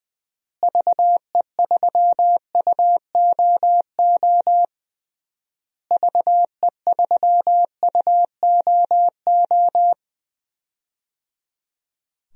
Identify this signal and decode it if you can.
Morse code, heard as a keyed tone, about 20 wpm, 700 Hz. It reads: VE3UOO VE3UOO
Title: DXCC-100 Call Signs - 6 Characters - Repeated 3x - 20wpm